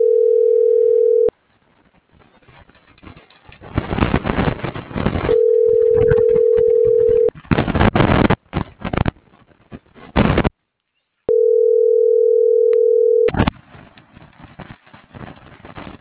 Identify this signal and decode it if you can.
Ringback tone (North America), 440 + 480 Hz.